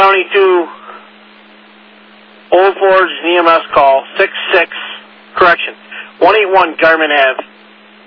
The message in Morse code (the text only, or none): none